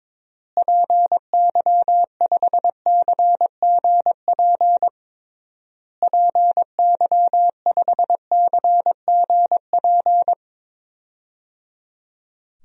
Morse code, heard as a keyed tone, about 22 wpm, 700 Hz. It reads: PY5CGP PY5CGP
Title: DXCC-100 Call Signs - 6 Characters - Repeated 3x - 22wpm